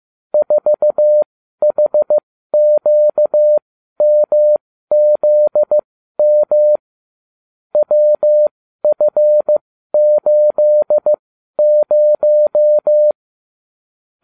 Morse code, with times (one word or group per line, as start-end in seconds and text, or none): 0.3-6.8 4HQMZM
7.7-13.1 WF80